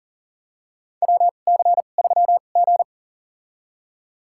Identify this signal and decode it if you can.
Morse code, heard as a keyed tone, about 38 wpm, spaced as for 28 wpm, 700 Hz. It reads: WC3G
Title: USA Call Signs - 4 Characters - 2x Character Spacing - 40wpm